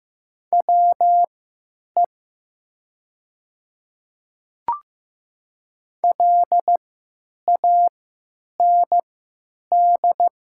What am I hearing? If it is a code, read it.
Morse code, heard as a keyed tone, about 15 wpm, spaced as for 8 wpm, 700 Hz. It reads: WE LAND
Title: 3x Character Spacing - Top 200 Words - 15wpm